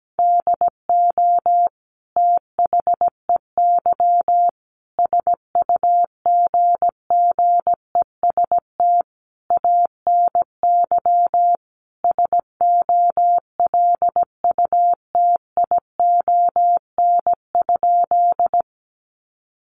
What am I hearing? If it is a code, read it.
Morse code, heard as a keyed tone, about 17 wpm, 700 Hz. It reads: DO THEY SUGGEST ANY SOLUTION?
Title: Sentences from Top 1300 Words 17wpm